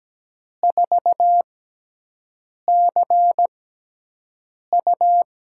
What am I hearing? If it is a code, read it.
Morse code, heard as a keyed tone, about 17 wpm, spaced as for 6 wpm, 700 Hz. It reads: 4CU
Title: USA Call Signs - 4 Characters - 6x Character Spacing - 17wpm